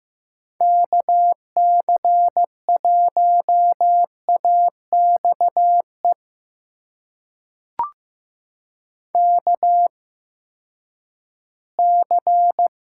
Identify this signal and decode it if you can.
Morse code, heard as a keyed tone, about 15 wpm, spaced as for 10 wpm, 700 Hz. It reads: KC1AXE KC
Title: USA Call Signs - 6 Characters - 8x Character Spacing - 15wpm